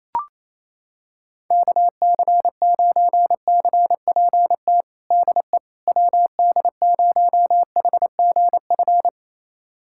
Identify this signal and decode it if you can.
Morse code, heard as a keyed tone, about 28 wpm, 700 Hz. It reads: KC9CPT DE WD0HGF
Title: USA Callsign DE Callsign - 6 Characters - 28wpm